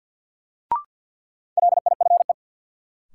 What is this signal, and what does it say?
Morse code, heard as a keyed tone, about 53 wpm, 700 Hz. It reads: LIFE
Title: Sentences from Top 300 Words - New Word Drill - 50wpm